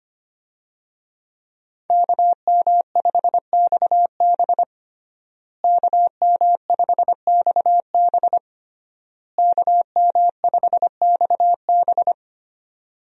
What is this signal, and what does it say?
Morse code, heard as a keyed tone, about 25 wpm, 700 Hz. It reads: KM5XB KM5XB KM5XB
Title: USA Call Signs - 5 Characters - Repeated 3x - 25wpm